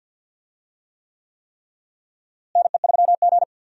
Morse code, heard as a keyed tone, about 51 wpm, 700 Hz. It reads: NE3G
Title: USA Call Signs - 4 Characters - Repeated 3x - 50wpm